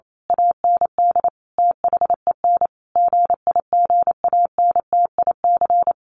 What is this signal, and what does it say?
Morse code, heard as a keyed tone, about 28 wpm, 700 Hz. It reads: AND THEN GIGANTIC
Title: A Sound Of Thunder 28wpm